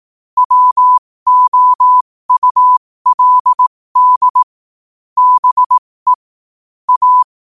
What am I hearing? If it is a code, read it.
Morse code, heard as a keyed tone, about 18 wpm, spaced as for 15 wpm, 1000 Hz. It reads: WOULD BE A
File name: qotdfiles / 15wpm